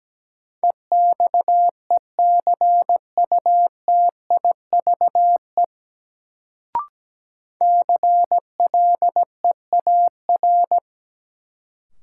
Morse code, heard as a keyed tone, about 17 wpm, 700 Hz. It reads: EXECUTIVE CLEAR
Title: Sets of 5 Words Encore - New Word Drill - 17wpm